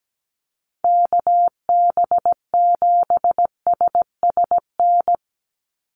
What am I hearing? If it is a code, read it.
Morse code, heard as a keyed tone, about 17 wpm, 700 Hz. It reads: KB7SSN
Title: USA Call Signs 17wpm